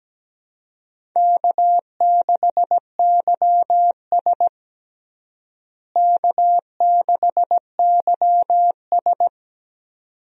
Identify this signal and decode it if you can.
Morse code, heard as a keyed tone, about 17 wpm, 700 Hz. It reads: K6YS K6YS
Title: USA Call Signs - 4 Characters - Repeated 2x - 17wpm